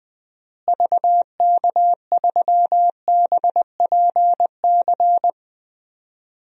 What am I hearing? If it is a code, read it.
Morse code, heard as a keyed tone, about 20 wpm, 700 Hz. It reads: VK3BPC
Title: DXCC-100 Call Signs - 6 Characters - Repeated 3x - 20wpm